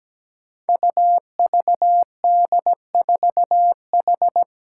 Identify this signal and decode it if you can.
Morse code, heard as a keyed tone, about 17 wpm, 700 Hz. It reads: UVD4H
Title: Five Challenging Letter-Number Combinations 17wpm